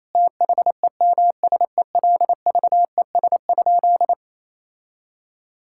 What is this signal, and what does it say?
Morse code, heard as a keyed tone, about 27 wpm, 700 Hz. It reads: THEMSELVES?
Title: Sentences from Top 1600 Words 28wpm